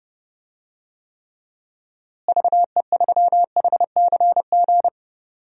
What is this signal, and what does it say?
Morse code, heard as a keyed tone, about 30 wpm, 700 Hz. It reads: VE3HCG
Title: DXCC-100 Call Signs - 6 Characters - Repeated 3x - 30wpm